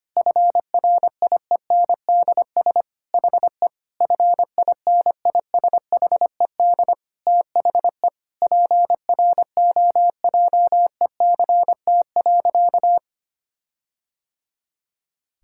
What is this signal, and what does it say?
Morse code, heard as a keyed tone, about 25 wpm, 700 Hz. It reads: FRIENDS HE FINISHED THE PROJECT.
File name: practice / 25wpm / Sentences with 4-Word Phrases - 25wpm